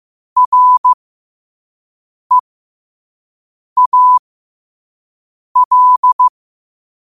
Morse code, heard as a keyed tone, about 15 wpm, spaced as for 5 wpm, 1000 Hz. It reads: REAL
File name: qotdfiles / 5wpm